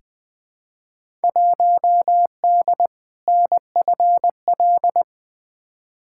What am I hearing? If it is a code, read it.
Morse code, heard as a keyed tone, about 20 wpm, 700 Hz. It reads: 1D NFL
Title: Field Day Exchanges - 20wpm